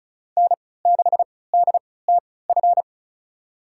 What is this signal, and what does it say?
Morse code, heard as a keyed tone, about 34 wpm, spaced as for 20 wpm, 700 Hz. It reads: N6DTF
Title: USA Call Signs - 5 Characters - 3x Character Spacing - 35wpm